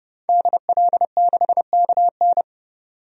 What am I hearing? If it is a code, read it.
Morse code, heard as a keyed tone, about 30 wpm, 700 Hz. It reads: DL6KN
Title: DXCC-100 Call Signs - 5 Characters - 30wpm